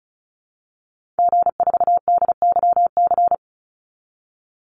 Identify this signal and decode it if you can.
Morse code, heard as a keyed tone, about 34 wpm, 700 Hz. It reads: G4DYC
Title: Call Signs Repeated 2x - 35wpm